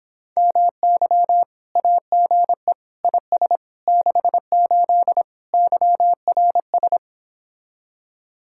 Morse code, heard as a keyed tone, about 26 wpm, 700 Hz. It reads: MY AGE IS 68 YRS